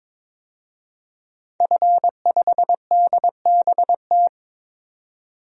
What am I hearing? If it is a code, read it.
Morse code, heard as a keyed tone, about 22 wpm, 700 Hz. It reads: F5DBT
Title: DXCC-100 Call Signs - 5 Characters - 22wpm